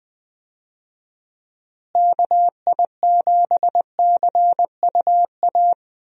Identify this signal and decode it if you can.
Morse code, heard as a keyed tone, about 20 wpm, 700 Hz. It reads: KI7CUA
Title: USA Call Signs - 6 Characters - Repeated 2x - 20wpm